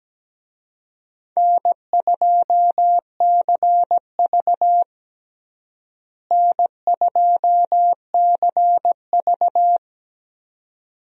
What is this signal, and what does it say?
Morse code, heard as a keyed tone, about 17 wpm, 700 Hz. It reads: N2CV N2CV
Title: USA Call Signs - 4 Characters - Repeated 2x - 17wpm